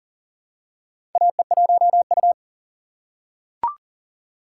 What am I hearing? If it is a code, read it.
Morse code, heard as a keyed tone, about 37 wpm, 700 Hz. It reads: AE1U
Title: USA Call Signs - 4 Characters - 3x Character Spacing - 40wpm